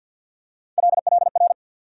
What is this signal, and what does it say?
Morse code, heard as a keyed tone, about 49 wpm, 700 Hz. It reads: LLR
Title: USA Call Sign Postfixes - 3 Letters 50wpm